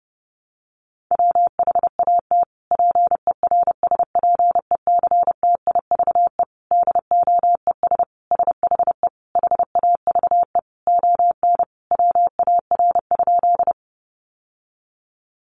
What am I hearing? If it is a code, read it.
Morse code, heard as a keyed tone, about 30 wpm, 700 Hz. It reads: WHAT PERSPECTIVE DOES SHE HAVE ON WAR?
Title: Sentences from Top 1200 Words 30wpm